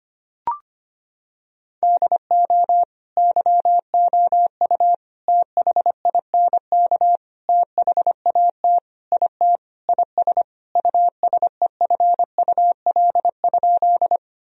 Morse code, heard as a keyed tone, about 25 wpm, 700 Hz. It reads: DO YOU THINK THAT IT IS USEFUL?
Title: Sentences from Top 1000 Words - Review Part 4 - 25wpm